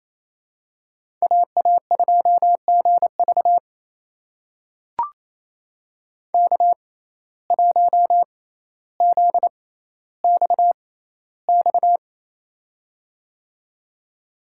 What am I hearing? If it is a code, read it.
Morse code, heard as a keyed tone, about 28 wpm, spaced as for 14 wpm, 700 Hz. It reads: AA2GV K1ZXX
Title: USA Call Signs - 5 Characters - 6x Character Spacing - 28wpm